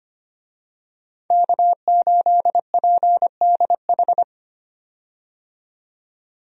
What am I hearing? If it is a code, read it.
Morse code, heard as a keyed tone, about 25 wpm, 700 Hz. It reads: K8PDH
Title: USA Call Signs - 5 Characters - 25wpm